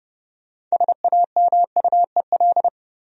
Morse code, heard as a keyed tone, about 30 wpm, 700 Hz. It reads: SAMUEL
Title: USA Common Names - 30wpm